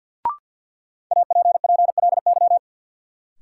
Morse code, heard as a keyed tone, about 53 wpm, 700 Hz. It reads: APPLY
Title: Sentences from Top 1200 Words - New Word Drill Alt - 50wpm